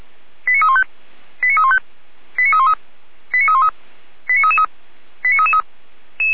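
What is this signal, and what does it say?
Five-tone selective call (ZVEI), heard as quick runs of five tones, tones of about 70 ms. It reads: 89315 89315 89313 89313 89333 89333
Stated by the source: Five-tone ZVEI selective call received on SDR.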